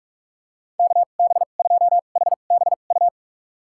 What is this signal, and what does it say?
Morse code, heard as a keyed tone, about 43 wpm, spaced as for 32 wpm, 700 Hz. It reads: KB2HBU
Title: USA Call Signs - 6 Characters - 2x Character Spacing - 45wpm